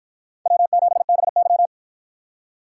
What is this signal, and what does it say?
Morse code, heard as a keyed tone, about 53 wpm, 700 Hz. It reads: W7BY